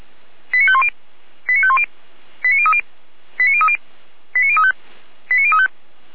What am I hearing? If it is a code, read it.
Five-tone selective call (ZVEI), heard as quick runs of five tones, tones of about 70 ms. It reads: 89510 89510 89030 89030 89035 89035
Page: SDR/ZVEI